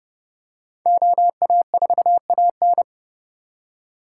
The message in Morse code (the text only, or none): OA4AN